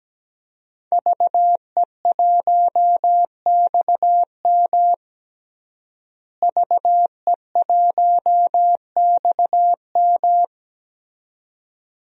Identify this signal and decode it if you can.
Morse code, heard as a keyed tone, about 17 wpm, 700 Hz. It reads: VE1XM VE1XM
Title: DXCC-100 Call Signs - 5 Characters - Repeated 2x - 17wpm